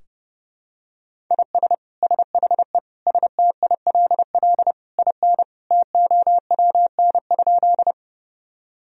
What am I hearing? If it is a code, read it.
Morse code, heard as a keyed tone, about 30 wpm, 700 Hz. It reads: IS SHE STILL IN TOWN?
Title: Sentences from Top 1000 Words - Review Part 2 - 30wpm